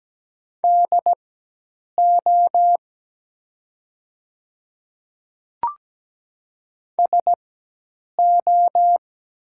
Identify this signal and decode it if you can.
Morse code, heard as a keyed tone, about 17 wpm, spaced as for 8 wpm, 700 Hz. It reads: DO SO
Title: 4x Character Spacing - Top 100 Words - 17wpm